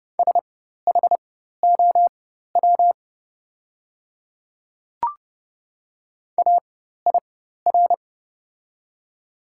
Morse code, heard as a keyed tone, about 31 wpm, spaced as for 14 wpm, 700 Hz. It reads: SHOW AIR
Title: 4x Character Spacing - Top 200 Words - 30wpm